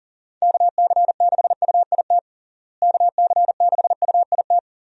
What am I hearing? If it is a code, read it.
Morse code, heard as a keyed tone, about 40 wpm, 700 Hz. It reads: KC6UIT KC6UIT
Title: USA Call Signs - 6 Characters - Repeated 2x - 40wpm